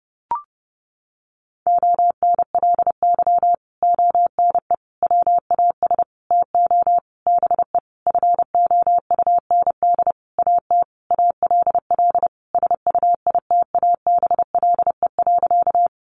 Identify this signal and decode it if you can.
Morse code, heard as a keyed tone, about 30 wpm, 700 Hz. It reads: ONLY ONE WAS TO BE FOUND AT ALL SUITABLE.
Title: Sentences from Top 1400 Words 30wpm